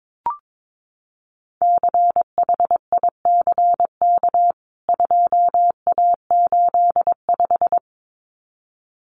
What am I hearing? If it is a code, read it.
Morse code, heard as a keyed tone, about 22 wpm, 700 Hz. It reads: CHICK 2A85
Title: CWT Exchange 22wpm